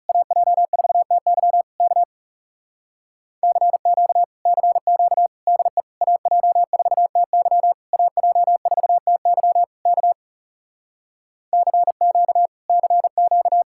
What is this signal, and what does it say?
Morse code, heard as a keyed tone, about 40 wpm, 700 Hz. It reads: AJ4TY K CQ CQ DE AJ4TY AJ4TY K CQ CQ